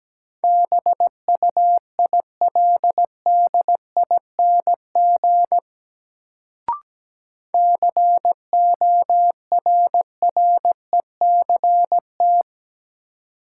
Morse code, heard as a keyed tone, about 17 wpm, 700 Hz. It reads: BUILDING CORRECT
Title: Sentences from Top 700 Words - New Word Drill - 17wpm